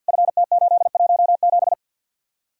Morse code, heard as a keyed tone, about 64 wpm, 700 Hz. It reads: FT817